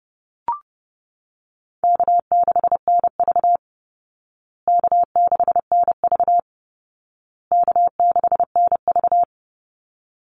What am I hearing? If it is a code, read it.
Morse code, heard as a keyed tone, about 28 wpm, 700 Hz. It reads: K6NV K6NV K6NV
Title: Easy Call Signs Repeated 3x - 30wpm